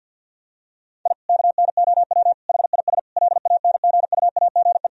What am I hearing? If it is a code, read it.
Morse code, heard as a keyed tone, about 51 wpm, 700 Hz. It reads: I KNOW HIS LANGUAGE
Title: Sentences from Top 1000 Words - Review Part 3 - 50wpm